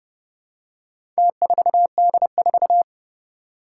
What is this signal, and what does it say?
Morse code, heard as a keyed tone, about 30 wpm, 700 Hz. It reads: T4D4
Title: Four Challenging Letter-Number Combinations 30wpm